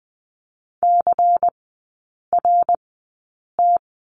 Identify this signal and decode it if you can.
Morse code, heard as a keyed tone, about 20 wpm, spaced as for 8 wpm, 700 Hz. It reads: CRT
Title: 114 Lesson 19e - Letter w - Three Characters 20wpm